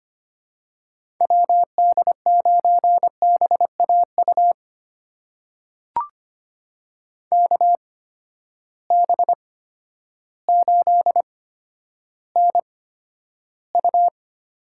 Morse code, heard as a keyed tone, about 25 wpm, spaced as for 11 wpm, 700 Hz. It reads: WD9BAU KB8NU
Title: USA Call Signs - 6 Characters - 8x Character Spacing - 25wpm